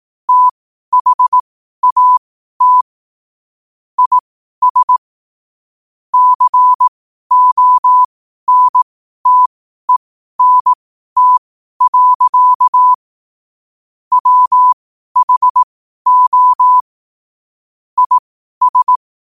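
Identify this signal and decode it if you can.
Morse code, heard as a keyed tone, about 18 wpm, spaced as for 12 wpm, 1000 Hz. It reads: THAT IS CONTENT. WHO IS
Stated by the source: Morse code sent at 12 words per minute.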